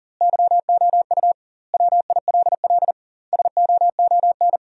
Morse code, heard as a keyed tone, about 40 wpm, 700 Hz. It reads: YOU WILL SOON
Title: 2x Word-Spacing - Sentences from Top 400 Words - 40wpm